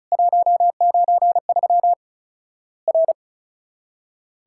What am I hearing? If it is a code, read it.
Morse code, heard as a keyed tone, about 36 wpm, 700 Hz. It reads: 193 R
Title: Worked All Europe DX Contest - QTC Exchange - 35wpm